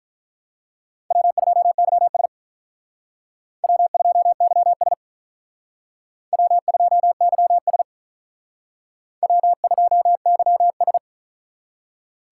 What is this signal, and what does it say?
Morse code, heard as a keyed tone, about 44 wpm, 700 Hz. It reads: W2YS W2YS W2YS W2YS